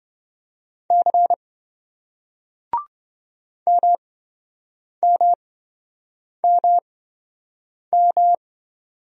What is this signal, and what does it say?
Morse code, heard as a keyed tone, about 27 wpm, spaced as for 7 wpm, 700 Hz. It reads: C MMMM